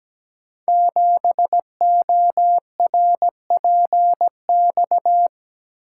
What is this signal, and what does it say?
Morse code, heard as a keyed tone, about 17 wpm, 700 Hz. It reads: 7ORPX
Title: Code Groups - 5 Letter:Numbers - 17wpm